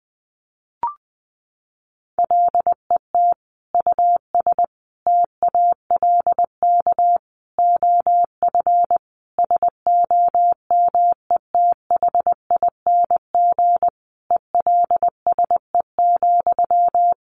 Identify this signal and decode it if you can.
Morse code, heard as a keyed tone, about 20 wpm, 700 Hz. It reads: LET US TALK OF SOMETHING ELSE,
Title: Sentences from Top 800 Words 20wpm